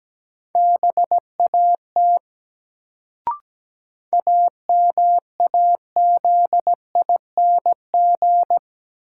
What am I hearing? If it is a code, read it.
Morse code, heard as a keyed tone, about 17 wpm, 700 Hz. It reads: BAT AMAZING
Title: Sentences from Top 1500 Words - New Word Drill - 17wpm